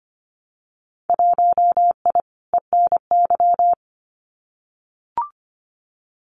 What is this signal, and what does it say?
Morse code, heard as a keyed tone, about 25 wpm, 700 Hz. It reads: 1I ENY
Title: Winter Field Day Exchanges Encore 25wpm